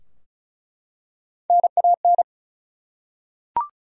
Morse code, heard as a keyed tone, about 36 wpm, 700 Hz. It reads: NAN
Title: USA Common Names - Max 4 Letters - 35wpm